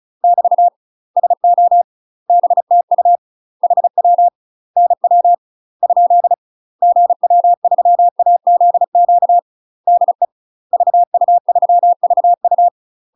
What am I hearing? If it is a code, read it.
Morse code, heard as a keyed tone, about 34 wpm, 700 Hz. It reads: <BT> SO BTU HW NW ? GW3AZQ DE VU3VU
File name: UK MASTER QSOs TRAINING / 35WPM / UK MASTER QSOs 0000